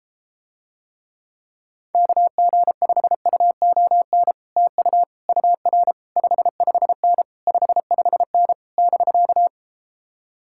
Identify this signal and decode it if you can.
Morse code, heard as a keyed tone, about 33 wpm, 700 Hz. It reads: KG5UON TU UR 55N 55N <BK>